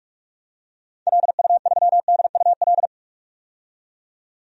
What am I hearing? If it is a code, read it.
Morse code, heard as a keyed tone, about 46 wpm, 700 Hz. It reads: LU3DUL
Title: DXCC-100 Call Signs - 6 Characters - 45wpm